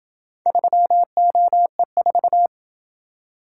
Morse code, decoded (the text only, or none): 3OE4